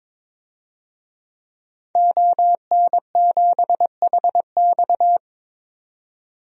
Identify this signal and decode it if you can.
Morse code, heard as a keyed tone, about 21 wpm, 700 Hz. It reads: ON7HX
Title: DXCC-100 Call Signs - 5 Characters - 22wpm